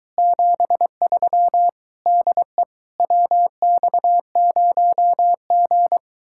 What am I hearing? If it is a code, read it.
Morse code, heard as a keyed tone, about 23 wpm, 700 Hz. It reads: 73 DE WX0G